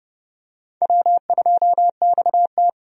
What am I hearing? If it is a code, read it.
Morse code, heard as a keyed tone, about 31 wpm, 700 Hz. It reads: W2XT
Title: USA Call Signs - 4 Characters - 30wpm